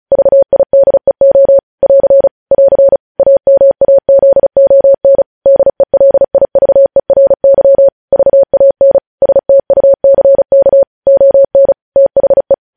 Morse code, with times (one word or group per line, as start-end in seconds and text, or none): 0.1-1.6 VIDEO
1.8-2.3 <AR>
2.5-3.0 <AR>
3.2-5.2 AMAZON
5.5-7.9 DELIVERY
8.1-9.0 VAN
9.2-10.8 STUCK
11.1-11.7 ON
12.0-12.5 THE